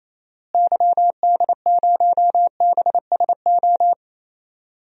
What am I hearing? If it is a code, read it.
Morse code, heard as a keyed tone, about 28 wpm, 700 Hz. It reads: YD0BSO
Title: DXCC-100 Call Signs - 6 Characters - Repeated 3x - 28wpm